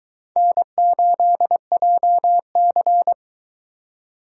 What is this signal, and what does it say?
Morse code, heard as a keyed tone, about 22 wpm, 700 Hz. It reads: N8JC